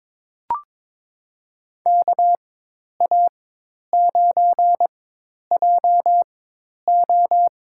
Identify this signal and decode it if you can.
Morse code, heard as a keyed tone, about 23 wpm, spaced as for 10 wpm, 700 Hz. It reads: KA9JO
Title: USA Call Signs - 4x Character Spacing - 22wpm